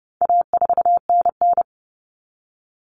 Morse code, heard as a keyed tone, about 30 wpm, 700 Hz. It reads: A4NN
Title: Common Cut Numbers - 4-Digit 30wpm